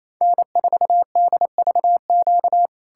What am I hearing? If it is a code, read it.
Morse code, heard as a keyed tone, about 38 wpm, 700 Hz. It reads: N4DVQ